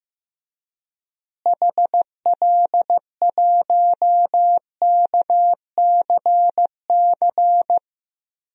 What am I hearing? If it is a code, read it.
Morse code, heard as a keyed tone, about 15 wpm, 700 Hz. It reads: HL1KCC
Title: DXCC-100 Call Signs - 6 Characters - Repeated 3x - 15wpm